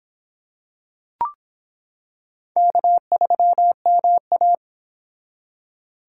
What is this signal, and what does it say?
Morse code, heard as a keyed tone, about 26 wpm, 700 Hz. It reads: K3MA